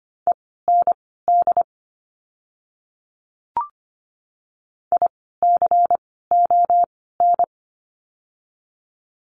Morse code, heard as a keyed tone, about 25 wpm, spaced as for 16 wpm, 700 Hz. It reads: END ICON
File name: course / 25:10wpm / 094 Lesson 16f - Letter c - Words 25wpm